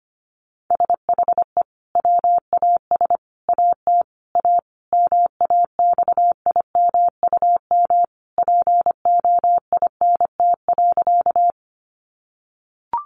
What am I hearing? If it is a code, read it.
Morse code, heard as a keyed tone, about 25 wpm, 700 Hz. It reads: SHE WAS AT A MAXIMUM POINT.
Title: Sentences from Top 1300 Words 25wpm